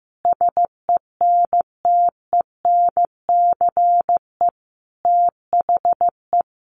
Morse code, heard as a keyed tone, about 15 wpm, 700 Hz. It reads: SENTENCE THE
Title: Additional Easy Phrases 15wpm